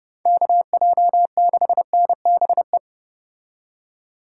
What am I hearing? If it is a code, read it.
Morse code, heard as a keyed tone, about 29 wpm, 700 Hz. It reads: KJ6NBE